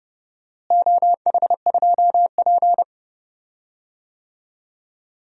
Morse code, heard as a keyed tone, about 30 wpm, 700 Hz. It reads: OH2P
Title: DXCC-100 Call Signs - 4 Characters - Repeated 3x - 30wpm